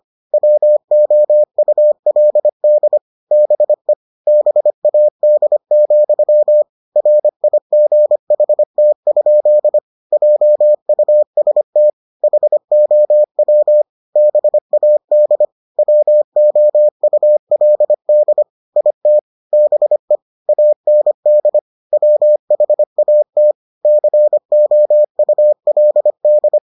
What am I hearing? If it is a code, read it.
Morse code, heard as a keyed tone, about 25 wpm, 600 Hz. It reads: WOULD BE BAD, RIGHT? JUST HOW BAD WOULD IT BE AND WHAT COULD